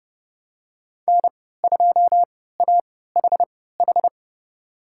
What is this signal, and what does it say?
Morse code, heard as a keyed tone, about 30 wpm, spaced as for 17 wpm, 700 Hz. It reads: N2AHH
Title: USA Call Signs - 5 Characters - 3x Character Spacing - 30wpm